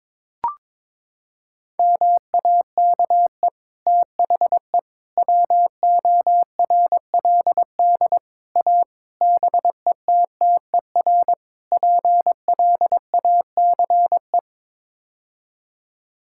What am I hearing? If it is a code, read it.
Morse code, heard as a keyed tone, about 22 wpm, 700 Hz. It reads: MAKE THE WORLD A BETTER PLACE
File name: practice / 22wpm / Sets of 6 Words - 22wpm